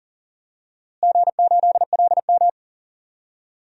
Morse code, heard as a keyed tone, about 40 wpm, 700 Hz. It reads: G8LM
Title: DXCC-100 Call Signs - 4 Characters - Repeated 3x - 40wpm